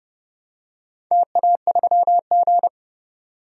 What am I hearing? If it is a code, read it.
Morse code, heard as a keyed tone, about 30 wpm, 700 Hz. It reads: TA3G
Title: DXCC-100 Call Signs - 4 Characters - 30wpm